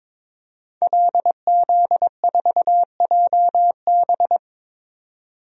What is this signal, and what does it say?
Morse code, heard as a keyed tone, about 23 wpm, 700 Hz. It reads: LZ4JB
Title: DXCC-100 Call Signs - 5 Characters - 22wpm